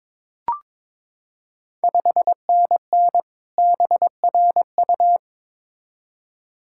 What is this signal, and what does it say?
Morse code, heard as a keyed tone, about 22 wpm, 690 Hz. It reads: 5NN BRU
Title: Ontario QSO Party Exchange 22wpm